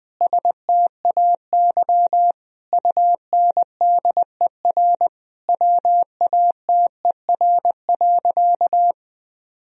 Morse code, heard as a keyed tone, about 20 wpm, 700 Hz. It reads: STAY UNDER WATER.
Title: Sentences from Top 1000 Words - Review Part 2 - 20wpm